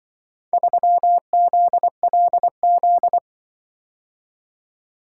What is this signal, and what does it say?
Morse code, heard as a keyed tone, about 24 wpm, 700 Hz. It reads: 3ZLZ
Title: Speed-Racing - Four Letter-Number Combinations - 20wpm